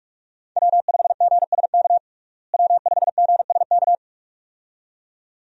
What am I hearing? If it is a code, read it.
Morse code, heard as a keyed tone, about 41 wpm, 700 Hz. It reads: W5GSK W5GSK
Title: USA Call Signs - 5 Characters - Repeated 2x - 45wpm